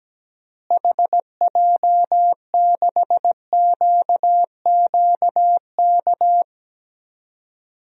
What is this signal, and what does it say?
Morse code, heard as a keyed tone, about 17 wpm, 700 Hz. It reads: HJ6QQK
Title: DXCC-100 Call Signs - 6 Characters - Repeated 3x - 17wpm